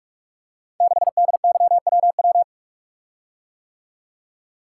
Morse code, heard as a keyed tone, about 44 wpm, 690 Hz. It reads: 6DYWW
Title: Code Groups - 5 Letter:Numbers - 45wpm